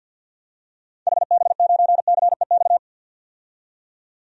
Morse code, heard as a keyed tone, about 52 wpm, 700 Hz. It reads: HB9CEX
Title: DXCC-100 Call Signs - 6 Characters - 50wpm